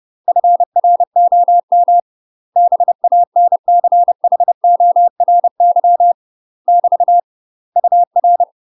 Morse code, heard as a keyed tone, about 30 wpm, 700 Hz. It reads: FROM BANCHORY <BT> UR
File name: UK MASTER QSOs TRAINING / 30WPM / UK MASTER QSOs 0015